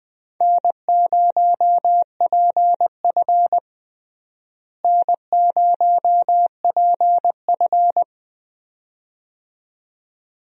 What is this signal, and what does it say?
Morse code, heard as a keyed tone, about 21 wpm, 700 Hz. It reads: N0PF N0PF
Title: USA Call Signs - 4 Characters - Repeated 2x - 20wpm